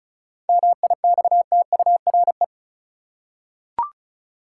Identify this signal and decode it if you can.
Morse code, heard as a keyed tone, about 34 wpm, 700 Hz. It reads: MIXTURE
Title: Seven Letter Words - 35wpm